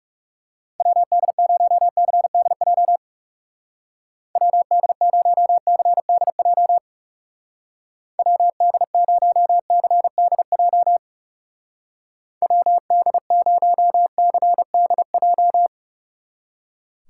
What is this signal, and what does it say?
Morse code, heard as a keyed tone, about 37 wpm, 700 Hz. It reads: WD0CDJ WD0CDJ WD0CDJ WD0CDJ
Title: Speed Racing - USA Call Signs 30wpm